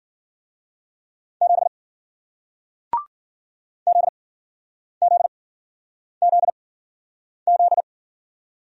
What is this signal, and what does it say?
Morse code, heard as a keyed tone, about 53 wpm, spaced as for 9 wpm, 700 Hz. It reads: 7 ZZZZ